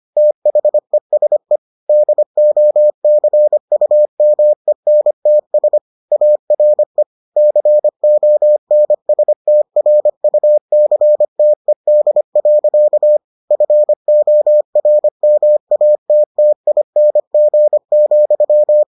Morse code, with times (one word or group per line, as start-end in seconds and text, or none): 0.2-1.6 THESE
1.9-5.8 DOCUMENTS
6.1-7.0 ARE
7.4-13.2 CONSTRUCTED.
13.5-18.8 FORMATTING,